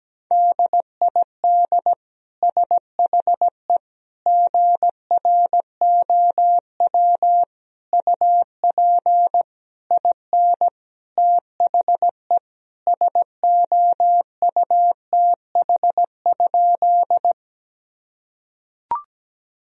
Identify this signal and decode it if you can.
Morse code, heard as a keyed tone, about 17 wpm, 700 Hz. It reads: DID SHE GROW UP IN THE SOUTH?
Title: Sentences from Top 1000 Words - Review Part 2 - 17wpm